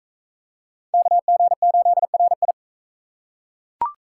Morse code, heard as a keyed tone, about 38 wpm, 700 Hz. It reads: KG8RI